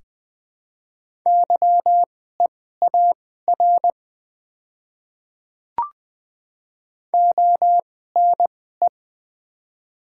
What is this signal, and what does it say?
Morse code, heard as a keyed tone, about 21 wpm, spaced as for 15 wpm, 700 Hz. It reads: YEAR ONE
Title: 2x Character Spacing - Top 200 Words - 20wpm